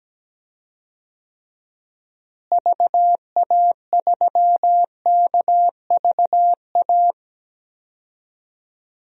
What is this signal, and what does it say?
Morse code, heard as a keyed tone, about 17 wpm, 700 Hz. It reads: VA3KVA
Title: DXCC-100 Call Signs - 6 Characters - Repeated 2x - 17wpm